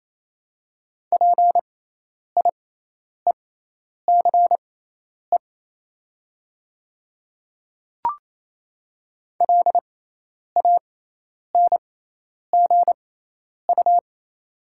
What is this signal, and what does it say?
Morse code, heard as a keyed tone, about 28 wpm, spaced as for 10 wpm, 700 Hz. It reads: PIECE LANGU
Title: 6x Character Spacing - Top 500 Words - 28wpm